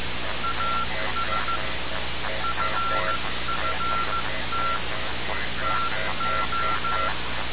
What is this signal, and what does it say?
Morse code, heard as a keyed tone, about 15 wpm, 1360 Hz, with noise. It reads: AS FFT TO
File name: radiocap / REC z